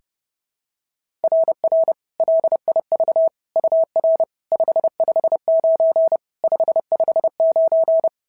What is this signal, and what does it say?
Morse code, heard as a keyed tone, about 30 wpm, 650 Hz. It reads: RR LIV UR 559 559
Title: SOTA QSO Practice 30wpm